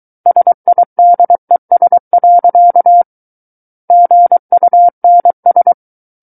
Morse code, heard as a keyed tone, about 23 wpm, 700 Hz. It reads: SIDES. GUNS
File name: Book One / 14-Chapter 14 - 23 WPM